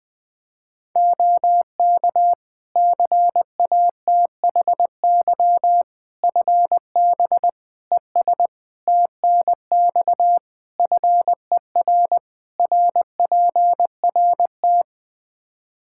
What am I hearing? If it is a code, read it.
Morse code, heard as a keyed tone, about 20 wpm, 700 Hz. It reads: OK CATHY FB ES TNX FER RPRT
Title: K1USN SST QSO Practice 20wpm